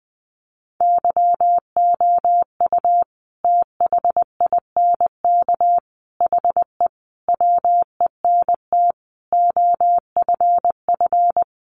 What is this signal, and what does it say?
Morse code, heard as a keyed tone, about 20 wpm, 700 Hz. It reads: YOU THINK HE WENT OFF
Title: Sentences from Top 800 Words 20wpm